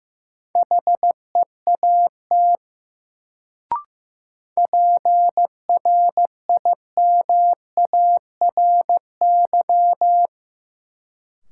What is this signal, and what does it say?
Morse code, heard as a keyed tone, about 15 wpm, 700 Hz. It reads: HEAT PRIMARY
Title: Sentences from Top 1100 Words - New Word Drill Alt - 15wpm